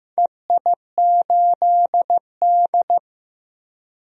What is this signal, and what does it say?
Morse code, heard as a keyed tone, about 15 wpm, 700 Hz. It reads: EI8D
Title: DXCC-100 Call Signs - 4 Characters - 15wpm